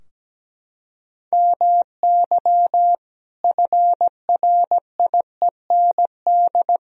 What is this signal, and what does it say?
Morse code, heard as a keyed tone, about 17 wpm, 700 Hz. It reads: MY FRIEND
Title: Sentences from Top 1000 Words - Review Part 4 - 17wpm